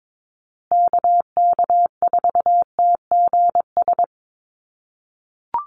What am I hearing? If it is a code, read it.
Morse code, heard as a keyed tone, about 22 wpm, 700 Hz. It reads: KK4TGS
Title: USA Call Signs Repeated 3x 22wpm